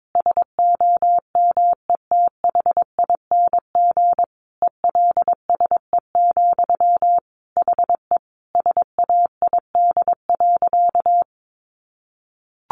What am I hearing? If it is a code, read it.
Morse code, heard as a keyed tone, about 22 wpm, 700 Hz. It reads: SOMETHING ELSE, HE SAID.
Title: Sentences from Top 800 Words 22wpm